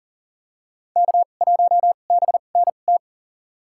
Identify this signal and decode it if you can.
Morse code, heard as a keyed tone, about 37 wpm, spaced as for 28 wpm, 700 Hz. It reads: K1BNT
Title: USA Call Signs - 5 Characters - 2x Character Spacing - 40wpm